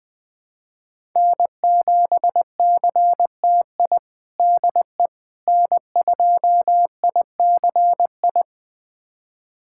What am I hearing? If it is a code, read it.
Morse code, heard as a keyed tone, about 20 wpm, 700 Hz. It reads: N7CTI DE N2ICI
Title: USA Callsign DE Callsign - 5 Characters - 20wpm